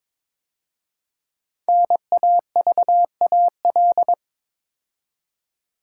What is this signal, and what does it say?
Morse code, heard as a keyed tone, about 23 wpm, 700 Hz. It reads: NAVAL
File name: 5 Letter Words - First 3 Letters Given - 22wpm